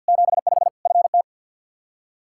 Morse code, heard as a keyed tone, about 52 wpm, 700 Hz. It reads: 75 FT
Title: K1USN SST QSO Practice 50wpm